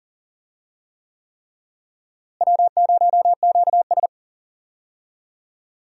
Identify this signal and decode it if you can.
Morse code, heard as a keyed tone, about 39 wpm, 700 Hz. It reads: W0QS
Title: USA Call Signs - 4 Characters - Repeated 3x - 40wpm